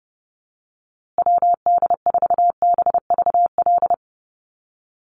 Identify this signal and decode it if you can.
Morse code, heard as a keyed tone, about 30 wpm, 700 Hz. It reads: WD4BVL
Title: USA Call Signs Repeated 2x 30wpm